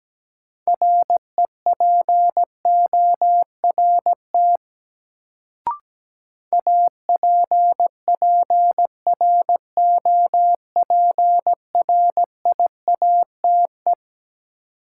Morse code, heard as a keyed tone, about 17 wpm, 700 Hz. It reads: REPORT APPROPRIATE
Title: Sentences from Top 900 Words - New Word Drill - 17wpm